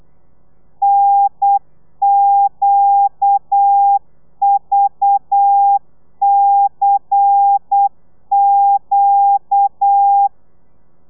Morse code, heard as a keyed tone, about 8 wpm, 800 Hz. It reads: NQVCQ
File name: telegrafia / i2tfj / Lez10 35